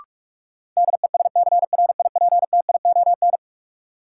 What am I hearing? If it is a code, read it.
Morse code, heard as a keyed tone, about 46 wpm, 700 Hz. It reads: DESCRIPTION
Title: Sentences from Top 1300 Words - New Word Drill - 45wpm